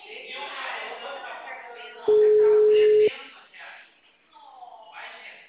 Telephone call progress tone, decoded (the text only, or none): ringback tone